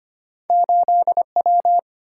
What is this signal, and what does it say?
Morse code, heard as a keyed tone, about 32 wpm, 700 Hz. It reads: 8W